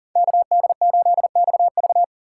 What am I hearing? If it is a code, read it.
Morse code, heard as a keyed tone, about 37 wpm, 700 Hz. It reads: KD8XV
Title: USA Call Signs - 5 Characters - 2x Character Spacing - 40wpm